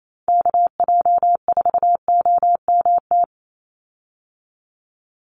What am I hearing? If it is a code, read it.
Morse code, heard as a keyed tone, about 28 wpm, 700 Hz. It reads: KJ4OMT
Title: USA Call Signs 28wpm